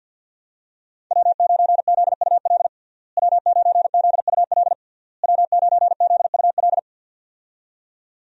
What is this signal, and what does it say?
Morse code, heard as a keyed tone, about 52 wpm, 700 Hz. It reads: W9ZUL W9ZUL W9ZUL
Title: USA Call Signs - 5 Characters - Repeated 3x - 50wpm